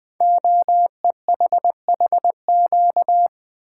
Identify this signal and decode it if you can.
Morse code, heard as a keyed tone, about 20 wpm, 700 Hz. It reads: OEHHQ
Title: Code Groups - 5 Letters - 20wpm